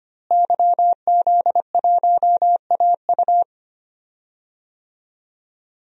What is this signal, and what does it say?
Morse code, heard as a keyed tone, about 25 wpm, 700 Hz. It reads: YZ1AU
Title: DXCC-100 Call Signs - 5 Characters - Repeated 2x - 25wpm